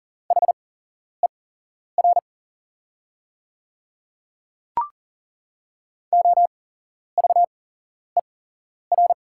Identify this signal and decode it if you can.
Morse code, heard as a keyed tone, about 40 wpm, spaced as for 11 wpm, 700 Hz. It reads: HER OVER
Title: 8x Character Spacing - Top 300 Words - 40wpm